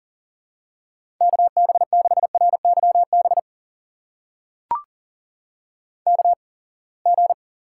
Morse code, heard as a keyed tone, about 40 wpm, spaced as for 28 wpm, 700 Hz. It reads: KB6RYB KG
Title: USA Call Signs - 6 Characters - 8x Character Spacing - 40wpm